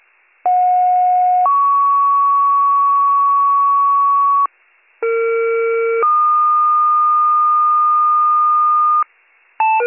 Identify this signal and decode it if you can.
Two-tone page, heard as a one-second tone, then a three-second tone, about 730 then 1120 Hz, about 460 then 1220 Hz. It is heard from 0.4 to 9.0 s.